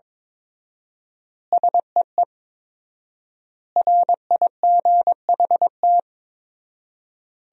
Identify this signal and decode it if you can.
Morse code, heard as a keyed tone, about 22 wpm, 700 Hz. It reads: SEE RIGHT
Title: 4x Word-Spacing - Sentences from Top 200 Words - 22wpm